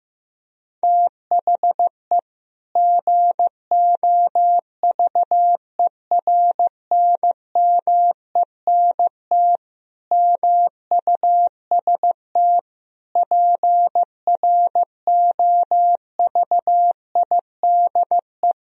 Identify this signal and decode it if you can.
Morse code, heard as a keyed tone, about 15 wpm, 700 Hz. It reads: THE GOVERNMENT MUST PROVIDE
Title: Sentences from Top 1000 Words - Review Part 3 - 15wpm